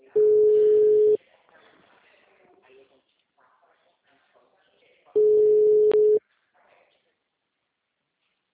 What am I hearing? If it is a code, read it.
Ringback tone (Europe, 425 Hz), with speech.
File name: out- -2007- - - .2886